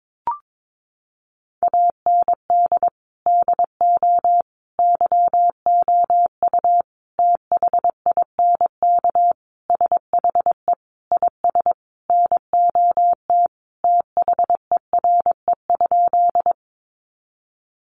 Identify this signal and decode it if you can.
Morse code, heard as a keyed tone, about 22 wpm, 700 Hz. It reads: AND DO YOU THINK SHE IS NOT THERE?
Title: Sentences from Top 600 Words 22wpm